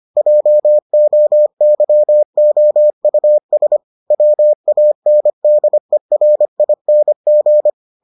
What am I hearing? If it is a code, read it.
Morse code, heard as a keyed tone, about 25 wpm, 600 Hz. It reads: JOYOUS WANDERING